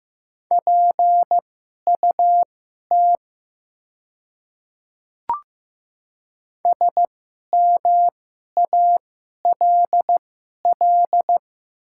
Morse code, heard as a keyed tone, about 15 wpm, spaced as for 11 wpm, 700 Hz. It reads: PUT SMALL
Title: 2x Character Spacing - Top 200 Words - 15wpm